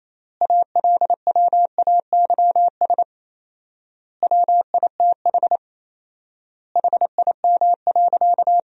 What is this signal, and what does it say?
Morse code, heard as a keyed tone, about 27 wpm, 700 Hz. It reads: ALWAYS WITH HIM.
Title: 4x Word-Spacing - Sentences from Top 400 Words - 28wpm